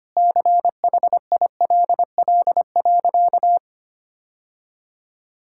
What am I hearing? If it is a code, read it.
Morse code, heard as a keyed tone, about 25 wpm, 700 Hz. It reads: CHILL.
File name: practice / 25wpm / Difficult Sentences with Hints - 25wpm